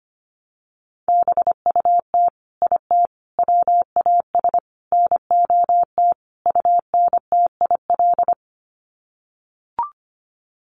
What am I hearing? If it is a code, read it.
Morse code, heard as a keyed tone, about 25 wpm, 700 Hz. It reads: BUT IT WAS NOT UNTIL
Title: Sets of 5 Words Encore2 25wpm